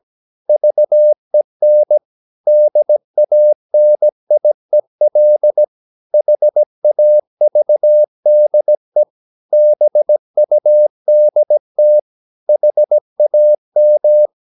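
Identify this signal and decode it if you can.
Morse code, heard as a keyed tone, about 17 wpm, 600 Hz. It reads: VEN DANIEL HAVDE BUDT HAM
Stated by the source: Slow (QRS) Morse practice at 17 wpm, from CW med Gnister.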